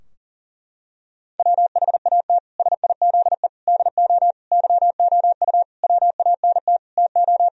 Morse code, heard as a keyed tone, about 40 wpm, 700 Hz. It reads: WHAT SIZE DO YOU WANT TO
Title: Sentences from Top 1000 Words - Review Part 2 - 40wpm